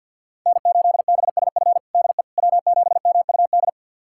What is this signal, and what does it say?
Morse code, heard as a keyed tone, about 52 wpm, 700 Hz. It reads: N8BSF DE W7MUD
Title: USA Callsign DE Callsign - 5 Characters - 50wpm